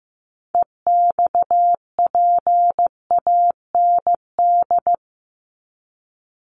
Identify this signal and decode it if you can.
Morse code, heard as a keyed tone, about 15 wpm, 700 Hz. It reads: EXPAND
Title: Top 1251-1500 Words 15wpm